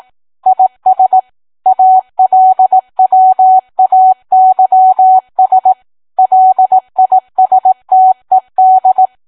Morse code, heard as a keyed tone, about 18 wpm, 750 Hz. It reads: IS ALWAYS LISTED